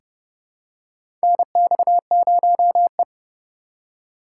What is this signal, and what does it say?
Morse code, heard as a keyed tone, about 30 wpm, 700 Hz. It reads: NX0E
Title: USA Call Signs - 4 Characters - 4x Character Spacing - 30wpm